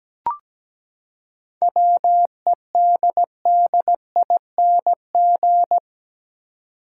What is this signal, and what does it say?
Morse code, heard as a keyed tone, about 17 wpm, 700 Hz. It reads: WEDDING
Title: Seven Letter Words - 17wpm